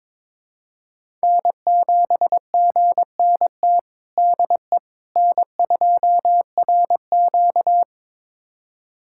Morse code, heard as a keyed tone, about 21 wpm, 700 Hz. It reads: N7GNT DE N2RQ
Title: USA Callsign DE Callsign 22wpm